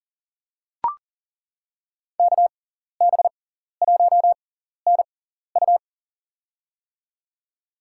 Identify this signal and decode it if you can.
Morse code, heard as a keyed tone, about 37 wpm, spaced as for 14 wpm, 700 Hz. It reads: KB1NU
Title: USA Call Signs - 5 Characters - 6x Character Spacing - 40wpm